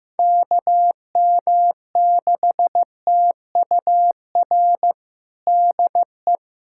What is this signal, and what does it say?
Morse code, heard as a keyed tone, about 15 wpm, 700 Hz. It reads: KM6TUR DE
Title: USA Callsign DE Callsign 15wpm